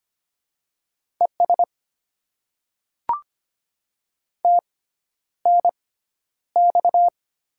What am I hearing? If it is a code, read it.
Morse code, heard as a keyed tone, about 25 wpm, spaced as for 11 wpm, 700 Hz. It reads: ES TNX
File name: Top 20 QSO Elements - 6x Character Spacing - 25wpm